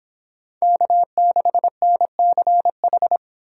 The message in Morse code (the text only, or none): K6NCH